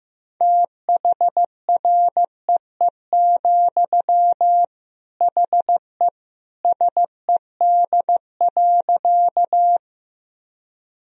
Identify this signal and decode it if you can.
Morse code, heard as a keyed tone, about 15 wpm, 700 Hz. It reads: THREE, HE SED.
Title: Sentences from Top 200 Words and Abbreviations 15wpm